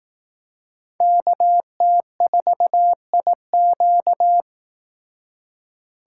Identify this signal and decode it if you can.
Morse code, heard as a keyed tone, about 18 wpm, 700 Hz. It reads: KT4IQ
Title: Speed Racing - USA 2x2 Call Signs 15wpm